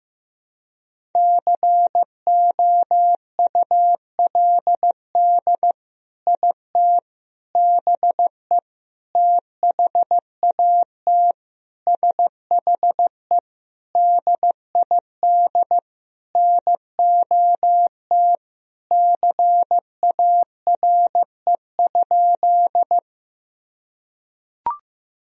Morse code, heard as a keyed tone, about 16 wpm, 700 Hz. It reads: COULD IT BE THAT SHE DID NOT CARE?
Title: Sentences from Top 1000 Words - Review Part 1 - 15wpm